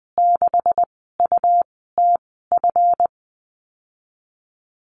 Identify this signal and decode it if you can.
Morse code, heard as a keyed tone, about 20 wpm, 700 Hz. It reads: 6 U T F
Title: 179 Lesson 29g - Slash - Call Signs 20wpm